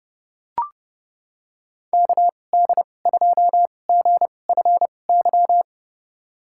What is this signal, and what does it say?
Morse code, heard as a keyed tone, about 30 wpm, spaced as for 22 wpm, 700 Hz. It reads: KD2GFY
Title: USA Call Signs - 2x Character Spacing - 30wpm